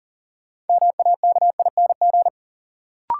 Morse code, heard as a keyed tone, about 40 wpm, 700 Hz. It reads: MAKING
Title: Sentences from Top 1000 Words - Review Part 6 - New Word Drill - 40wpm